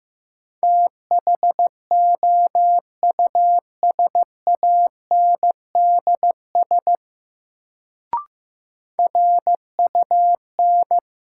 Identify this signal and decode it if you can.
Morse code, heard as a keyed tone, about 16 wpm, 700 Hz. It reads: THOUSANDS RUN
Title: Sets of 4 Words - New Word Drill - 15wpm